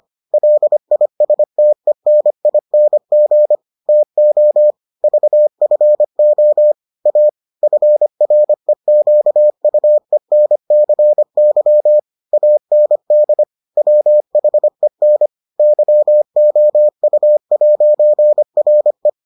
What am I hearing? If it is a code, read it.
Morse code, heard as a keyed tone, about 25 wpm, 600 Hz. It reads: LISTENING TO VFO A FREQUENCY AND WHEN YOU'RE